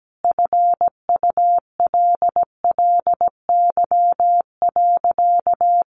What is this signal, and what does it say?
Morse code, heard as a keyed tone, about 17 wpm, 700 Hz. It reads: FULLY.
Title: Sentences from Top 1300 Words 17wpm